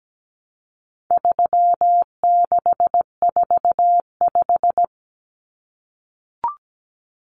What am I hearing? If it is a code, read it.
Morse code, heard as a keyed tone, about 17 wpm, 700 Hz. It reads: 3645
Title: Common Cut Numbers - 4-Digit 17wpm